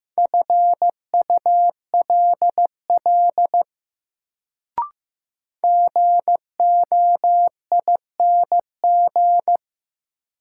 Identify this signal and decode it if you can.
Morse code, heard as a keyed tone, about 15 wpm, 700 Hz. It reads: FULL GOING
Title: Sentences from Top 500 Words - New Word Drill - 15wpm